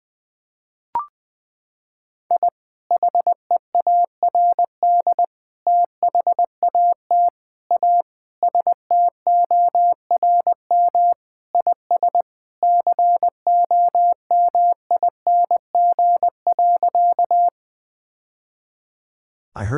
Morse code, heard as a keyed tone, about 21 wpm, 700 Hz. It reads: I HEARD THAT A STORM IS COMING.
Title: Sentences with 2-Word Phrases - 20wpm